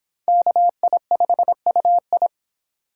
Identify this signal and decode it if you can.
Morse code, heard as a keyed tone, about 24 wpm, 700 Hz. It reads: KI5UI